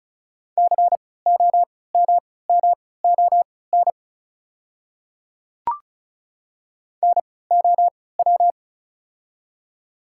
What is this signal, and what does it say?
Morse code, heard as a keyed tone, about 34 wpm, spaced as for 20 wpm, 690 Hz. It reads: COMMON NOW
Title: 3x Character Spacing - Top 500 Words - 35wpm